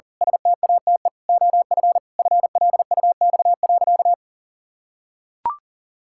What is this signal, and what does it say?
Morse code, heard as a keyed tone, about 40 wpm, 700 Hz. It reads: STATE OF FLUX.
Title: Pangram Challenge v2 - 40wpm